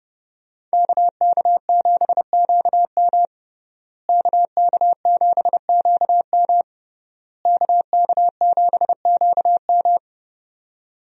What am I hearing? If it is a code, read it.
Morse code, heard as a keyed tone, about 30 wpm, 700 Hz. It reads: KK7QM KK7QM KK7QM
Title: USA Call Signs - 5 Characters - Repeated 3x - 30wpm